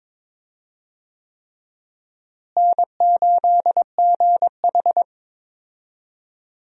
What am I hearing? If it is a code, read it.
Morse code, heard as a keyed tone, about 22 wpm, 700 Hz. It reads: N8GH
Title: USA Call Signs - 4 Characters - Repeated 3x - 22wpm